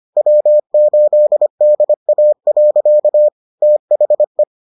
Morse code, heard as a keyed tone, about 25 wpm, 600 Hz. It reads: W8DA. THE